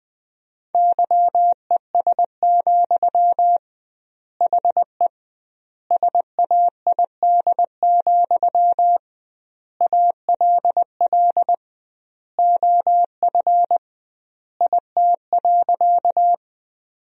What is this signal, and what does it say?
Morse code, heard as a keyed tone, about 20 wpm, 700 Hz. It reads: YES, HE SAID, ALL OF IT.
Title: 2x Word-Spacing - Sentences from Top 500 Words - 20wpm